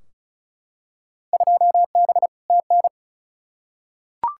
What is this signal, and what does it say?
Morse code, heard as a keyed tone, about 36 wpm, 700 Hz. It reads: 2B TN
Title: Field Day Exchanges Encore - 35wpm